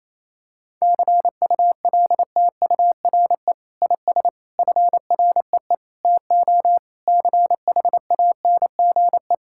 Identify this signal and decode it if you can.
Morse code, heard as a keyed tone, about 28 wpm, 700 Hz. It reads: CULTURE IS FREE TO CHANGE
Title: Sentences from Top 1000 Words - Review Part 6 - 28wpm